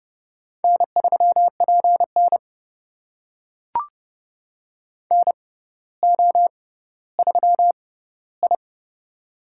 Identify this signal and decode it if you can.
Morse code, heard as a keyed tone, about 30 wpm, spaced as for 15 wpm, 700 Hz. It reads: N3PN NO3I
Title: USA Call Signs - 4 Characters - 6x Character Spacing - 30wpm